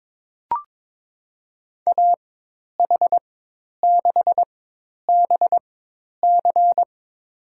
Morse code, heard as a keyed tone, about 23 wpm, spaced as for 10 wpm, 700 Hz. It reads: AH6BC
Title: USA Call Signs - 5 Characters - 4x Character Spacing - 22wpm